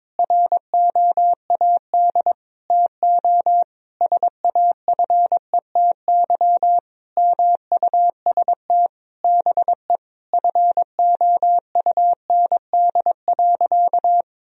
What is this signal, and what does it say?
Morse code, heard as a keyed tone, about 22 wpm, 700 Hz. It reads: ROAD TO SAFETY MUST BE FOUND.
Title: Sentences from Top 1000 Words - Review Part 4 - 22wpm